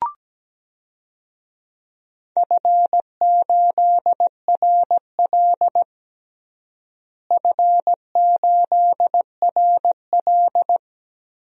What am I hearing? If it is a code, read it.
Morse code, heard as a keyed tone, about 17 wpm, 700 Hz. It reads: F8RL F8RL
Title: DXCC-100 Call Signs - 4 Characters - Repeated 2x - 17wpm